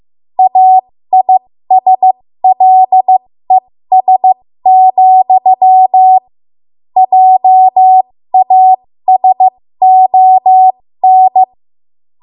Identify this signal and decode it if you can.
Morse code, heard as a keyed tone, about 15 wpm, spaced as for 13 wpm, 750 Hz. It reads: AISLES, JASON
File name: Morse / 13WPM